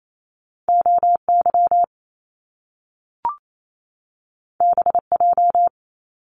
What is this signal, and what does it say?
Morse code, heard as a keyed tone, about 28 wpm, 700 Hz. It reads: O Y B J
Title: Top Two Letter Combinations in Words 28wpm